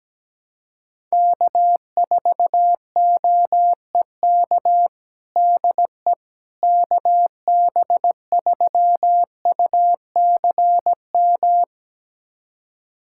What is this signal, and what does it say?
Morse code, heard as a keyed tone, about 17 wpm, 700 Hz. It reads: K4OEK DE KB3UCM
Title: USA Callsign DE Callsign 17wpm